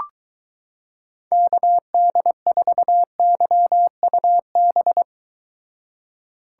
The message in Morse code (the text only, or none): KD4YUB